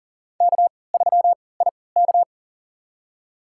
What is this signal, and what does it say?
Morse code, heard as a keyed tone, about 40 wpm, spaced as for 23 wpm, 700 Hz. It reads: K3IK
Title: USA Call Signs - 4 Characters - 3x Character Spacing - 40wpm